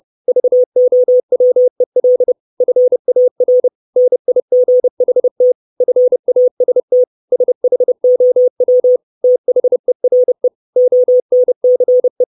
Morse code, heard as a keyed tone, about 30 wpm, 500 Hz. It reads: VOWEL FAR NIGHT FAST SHOW THERE ONCE
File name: 632 Words 30WPM